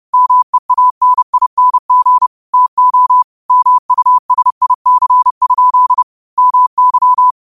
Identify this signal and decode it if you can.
Morse code, heard as a keyed tone, about 30 wpm, 1000 Hz. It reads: MEANING TO MUSIC? MY
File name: qotdfiles / 30wpm